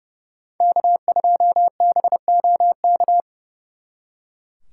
Morse code, heard as a keyed tone, about 38 wpm, 700 Hz. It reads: K2BOK